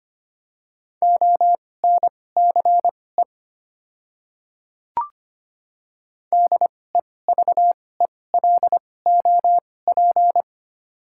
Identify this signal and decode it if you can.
Morse code, heard as a keyed tone, about 25 wpm, spaced as for 18 wpm, 700 Hz. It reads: ONCE DEVELOP
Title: 2x Character Spacing - Top 500 Words - 25wpm